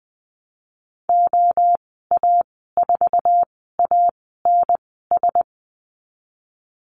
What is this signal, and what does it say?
Morse code, heard as a keyed tone, about 19 wpm, spaced as for 14 wpm, 700 Hz. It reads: OA4ANS
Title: 048 Lesson 9g - Number 4 - Call Signs 20wpm